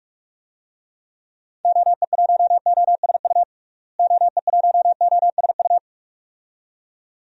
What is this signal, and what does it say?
Morse code, heard as a keyed tone, about 46 wpm, 700 Hz. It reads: OE1OSU OE1OSU
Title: DXCC-100 Call Signs - 6 Characters - Repeated 2x - 45wpm